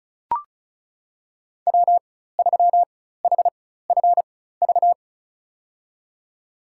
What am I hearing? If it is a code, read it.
Morse code, heard as a keyed tone, about 36 wpm, spaced as for 16 wpm, 700 Hz. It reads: W3HFV
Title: USA Call Signs - 5 Characters - 4x Character Spacing - 35wpm